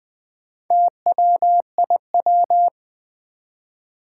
Morse code, heard as a keyed tone, about 20 wpm, 700 Hz. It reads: TWIW
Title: Four Challenging Letter-Number Combinations 20wpm